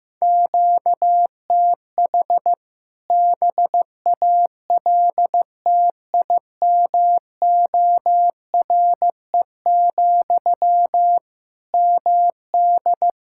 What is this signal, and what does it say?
Morse code, heard as a keyed tone, about 15 wpm, 700 Hz. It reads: QTH BALTIMORE, MD
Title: K1USN SST QSO Practice 15wpm